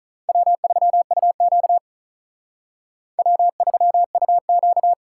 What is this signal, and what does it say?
Morse code, heard as a keyed tone, about 44 wpm, 700 Hz. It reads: W3UQ W3UQ